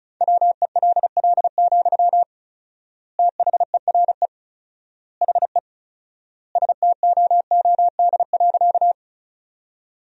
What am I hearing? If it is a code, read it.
Morse code, heard as a keyed tone, about 34 wpm, 700 Hz. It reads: WELL, THERE HE STOOD.
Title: 4x Word-Spacing - Sentences from Top 500 Words - 35wpm